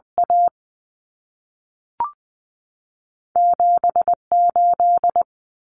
Morse code, heard as a keyed tone, about 20 wpm, spaced as for 3 wpm, 700 Hz. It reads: A7 8
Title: Common Cut Numbers - 2-Digit 20wpm